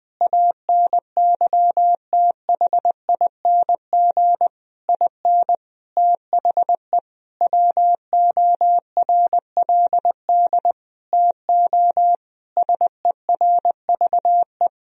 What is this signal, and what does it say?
Morse code, heard as a keyed tone, about 20 wpm, 700 Hz. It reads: ANYTHING IN THE WORLD TO SERVE
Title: Sentences from Top 1000 Words - Review Part 2 - 20wpm